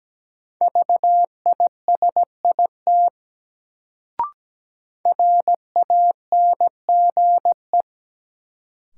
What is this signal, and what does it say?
Morse code, heard as a keyed tone, about 17 wpm, 700 Hz. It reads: VISIT RANGE
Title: Sentences from Top 900 Words - New Word Drill - 17wpm